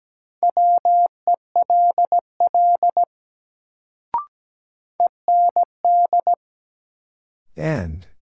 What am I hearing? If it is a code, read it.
Morse code, heard as a keyed tone, about 17 wpm, 700 Hz. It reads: WELL END
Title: Sentences from Top 200 Words - New Word Drill - 17wpm